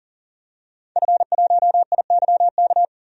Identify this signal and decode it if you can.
Morse code, heard as a keyed tone, about 40 wpm, 700 Hz. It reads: F1IYK
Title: DXCC-100 Call Signs - 5 Characters - 40wpm